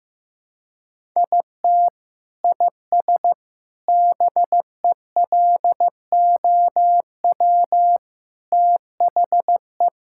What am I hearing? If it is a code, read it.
Morse code, heard as a keyed tone, about 15 wpm, 700 Hz. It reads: IT IS BELOW THE
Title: Sentences from Top 1000 Words - Review Part 5 - 15wpm